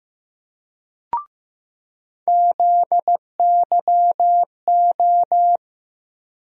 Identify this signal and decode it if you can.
Morse code, heard as a keyed tone, about 15 wpm, 700 Hz. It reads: ZYO
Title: USA Call Sign Postfixes - 3 Letters 15wpm